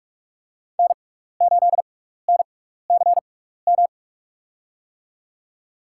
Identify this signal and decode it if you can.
Morse code, heard as a keyed tone, about 45 wpm, spaced as for 15 wpm, 700 Hz. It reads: N8NCM
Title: USA Call Signs - 5 Characters - 6x Character Spacing - 45wpm